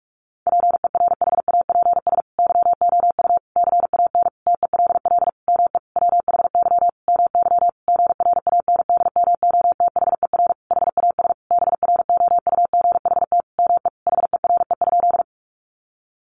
Morse code, heard as a keyed tone, about 44 wpm, 700 Hz. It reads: PERHAPS YOU CAN TELL ME WHY MY GRANDMOTHER HAS BROUGHT ME HERE?
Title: Sentences from Top 1200 Words 45wpm